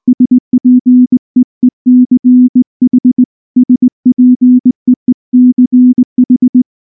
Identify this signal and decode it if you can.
Morse code, heard as a keyed tone, about 19 wpm, 250 Hz. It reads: SPEECH SPEECH
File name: morse / morse-training-20WPM-8